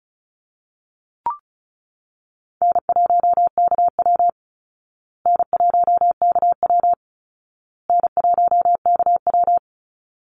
Morse code, heard as a keyed tone, about 34 wpm, 700 Hz. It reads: N1KW N1KW N1KW
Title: Easy Call Signs Repeated 3x - 35wpm